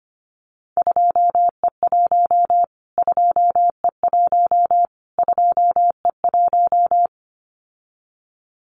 Morse code, heard as a keyed tone, about 25 wpm, 700 Hz. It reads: 2E1 2E1 2E1
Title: Callsign Prefixes for Top 25 DXCC - Repeated 3x - 25wpm